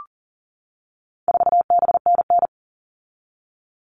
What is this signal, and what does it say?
Morse code, heard as a keyed tone, about 40 wpm, 700 Hz. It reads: 4BNN
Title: All Cut Numbers - 4-Digit 40wpm